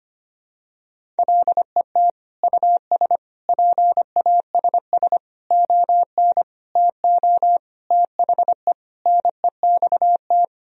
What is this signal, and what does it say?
Morse code, heard as a keyed tone, about 25 wpm, 700 Hz. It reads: LET US PASS ON TO THE NEXT
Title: Sentences from Top 1000 Words - Review Part 1 - 25wpm